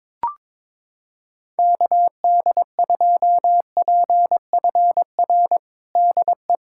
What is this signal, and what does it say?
Morse code, heard as a keyed tone, about 21 wpm, 700 Hz. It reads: KD2PFR DE
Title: USA Callsign DE Callsign - 6 Characters - 22wpm